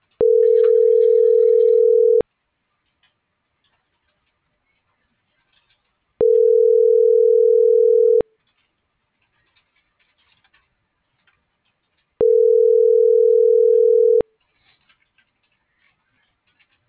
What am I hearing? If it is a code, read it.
Ringback tone (North America), 440 + 480 Hz.
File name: CENTRAL SOTOVA / out alejandro.rodriguez - .56047